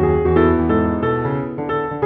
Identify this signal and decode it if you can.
There is no code.